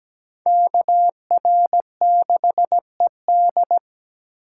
Morse code, heard as a keyed tone, about 20 wpm, 700 Hz. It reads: KR6ED
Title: Speed Racing - USA 2x2 Call Signs 17wpm